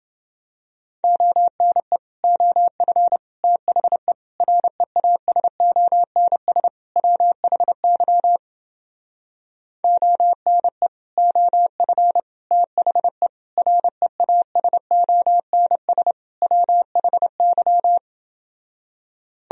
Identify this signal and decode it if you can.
Morse code, heard as a keyed tone, about 28 wpm, 700 Hz. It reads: ONE OF THE REASONS WHY ONE OF THE REASONS WHY